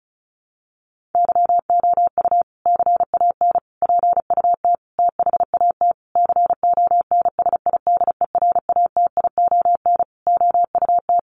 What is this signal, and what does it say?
Morse code, heard as a keyed tone, about 34 wpm, 700 Hz. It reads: YOU CAN PUT THAT CONSIDERATION OUT
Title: Sentences from Top 1500 Words 35wpm